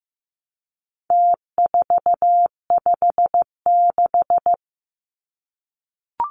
Morse code, heard as a keyed tone, about 15 wpm, 700 Hz. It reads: T456
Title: Numbers - Contest Style - 4-Digit 15wpm